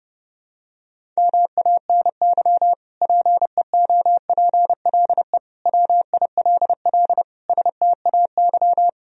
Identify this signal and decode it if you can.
Morse code, heard as a keyed tone, about 32 wpm, 700 Hz. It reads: MANY PEOPLE WILL STAY
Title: Sentences from Top 1000 Words - Review Part 1 - 30wpm